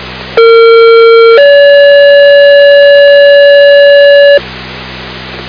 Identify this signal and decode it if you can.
Two-tone page, heard as a one-second tone, then a three-second tone, about 460 then 570 Hz, with noise. It is heard from 0.4 to 4.4 s.